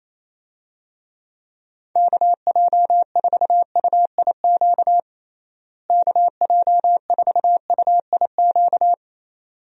Morse code, heard as a keyed tone, about 28 wpm, 700 Hz. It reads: KJ4UIQ KJ4UIQ
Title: USA Call Signs - 6 Characters - Repeated 2x - 28wpm